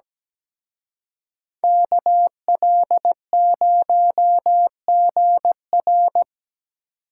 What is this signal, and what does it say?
Morse code, heard as a keyed tone, about 20 wpm, 700 Hz. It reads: KL0GR
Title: Speed Racing - USA 2x2 Call Signs 17wpm